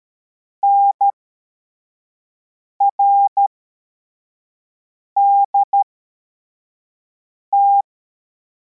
Morse code, heard as a keyed tone, about 13 wpm, spaced as for 4 wpm, 800 Hz. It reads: NRDT
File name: formation / cw / Text24